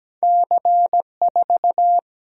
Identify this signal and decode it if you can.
Morse code, heard as a keyed tone, about 17 wpm, 700 Hz. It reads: C4
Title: Single-Letter Single-Number 17wpm